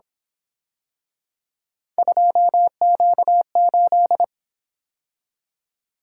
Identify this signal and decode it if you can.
Morse code, heard as a keyed tone, about 27 wpm, 700 Hz. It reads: 2Q8
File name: Speed-Racing - Three Letter-Number Combinations - 22wpm